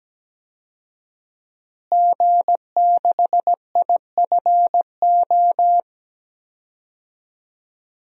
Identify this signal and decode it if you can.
Morse code, heard as a keyed tone, about 17 wpm, 700 Hz. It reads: G6IFO
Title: DXCC-100 Call Signs - 5 Characters - Repeated 3x - 17wpm